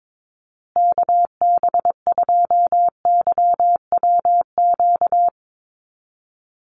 Morse code, heard as a keyed tone, about 22 wpm, 700 Hz. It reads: KB2YWQ
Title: USA Call Signs Repeated 2x 22wpm